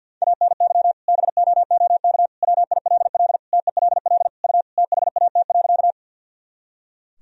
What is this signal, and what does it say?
Morse code, heard as a keyed tone, about 50 wpm, 700 Hz. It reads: ANY BOOK WILL TELL U THAT.
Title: Sentences from Top 300 Words and Abbreviations 50wpm